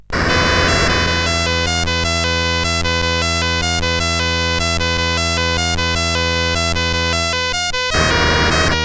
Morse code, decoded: none